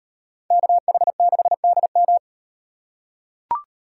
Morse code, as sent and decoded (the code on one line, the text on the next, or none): -.- .... -.... -.. --
KH6DM